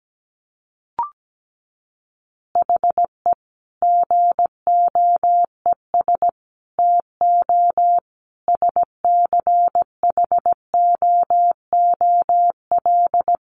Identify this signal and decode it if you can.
Morse code, heard as a keyed tone, about 17 wpm, 700 Hz. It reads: HE GOES TO SCHOOL
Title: Additional Easy Phrases 17wpm